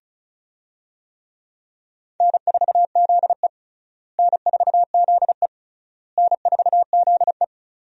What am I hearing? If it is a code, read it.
Morse code, heard as a keyed tone, about 36 wpm, 700 Hz. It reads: N4ZE N4ZE N4ZE
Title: USA Call Signs - 4 Characters - Repeated 3x - 35wpm